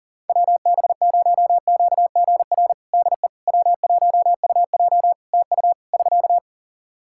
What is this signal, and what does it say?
Morse code, heard as a keyed tone, about 37 wpm, 700 Hz. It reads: WB0QGR DE W1UJ TU <SK>
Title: Practice QSO - Exchange 4 - Ending - 40wpm